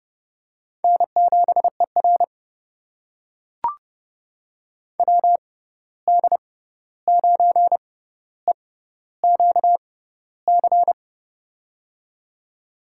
Morse code, heard as a keyed tone, about 32 wpm, spaced as for 14 wpm, 700 Hz. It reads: N7ER WD9EQC
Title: USA Call Signs - 6x Character Spacing - 30wpm